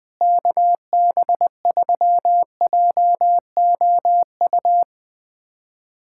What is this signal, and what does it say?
Morse code, heard as a keyed tone, about 20 wpm, 700 Hz. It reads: KB3JOU
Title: USA Callsign DE Callsign - 6 Characters - 20wpm